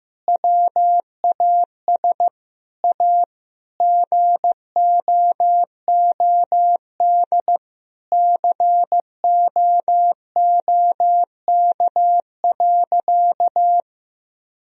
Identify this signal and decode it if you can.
Morse code, heard as a keyed tone, about 15 wpm, 700 Hz. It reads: WAS A GOOD COOK.
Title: Sentences from Top 1000 Words - Review Part 4 - 15wpm